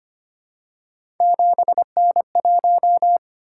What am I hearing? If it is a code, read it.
Morse code, heard as a keyed tone, about 25 wpm, 700 Hz. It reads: 7N1
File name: Three Letter-Number Combinations - 25wpm